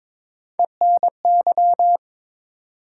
Morse code, heard as a keyed tone, about 22 wpm, 700 Hz. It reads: ENY
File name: ARRL Sections v2 - 22wpm